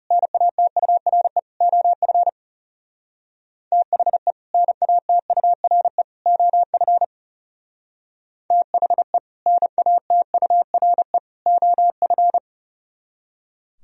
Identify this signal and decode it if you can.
Morse code, heard as a keyed tone, about 36 wpm, 700 Hz. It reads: NATURE OF THE NATURE OF THE NATURE OF
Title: Speed-Racing - Sets of 3 Words 30wpm